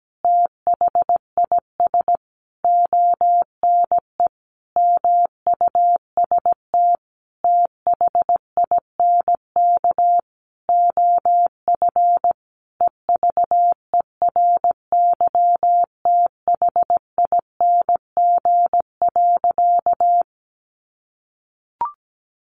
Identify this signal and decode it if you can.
Morse code, heard as a keyed tone, about 17 wpm, 700 Hz. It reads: THIS ONE MUST THINK OF EVERYTHING.
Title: Sentences from Top 1000 Words 17wpm